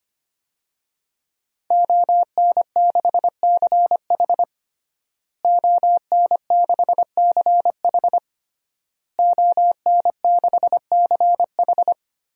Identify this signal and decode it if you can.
Morse code, heard as a keyed tone, about 25 wpm, 700 Hz. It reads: ON6CH ON6CH ON6CH
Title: DXCC-100 Call Signs - 5 Characters - Repeated 3x - 25wpm